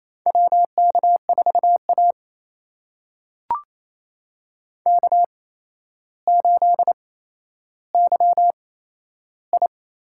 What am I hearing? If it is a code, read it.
Morse code, heard as a keyed tone, about 28 wpm, spaced as for 12 wpm, 700 Hz. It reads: WK4A K8YI
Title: USA Call Signs - 4 Characters - 8x Character Spacing - 28wpm